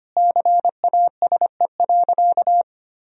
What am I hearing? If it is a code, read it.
Morse code, heard as a keyed tone, about 25 wpm, 700 Hz. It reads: CASE.
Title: Sentences from Top 1000 Words - Review Part 3 - 25wpm